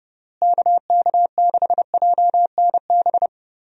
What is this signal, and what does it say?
Morse code, heard as a keyed tone, about 31 wpm, 700 Hz. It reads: KK6JNB
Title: USA Call Signs - 6 Characters - Repeated 3x - 30wpm